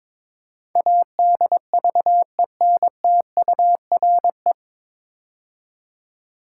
Morse code, heard as a keyed tone, about 22 wpm, 700 Hz. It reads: ADVENTURE
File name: 9 Letter Words - First 3 Letters Given - 22wpm